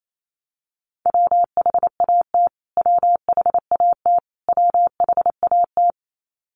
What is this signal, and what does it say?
Morse code, heard as a keyed tone, about 28 wpm, 700 Hz. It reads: WHAT WHAT WHAT
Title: Top 500 Words Repeated 28wpm